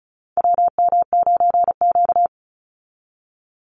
Morse code, heard as a keyed tone, about 34 wpm, 700 Hz. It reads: WM9Q
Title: Call Signs Repeated 3x - 35wpm